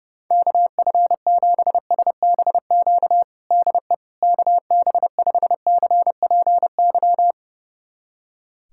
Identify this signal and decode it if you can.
Morse code, heard as a keyed tone, about 30 wpm, 700 Hz. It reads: KF7SBQ DE KB5CPY
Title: USA Callsign DE Callsign 30wpm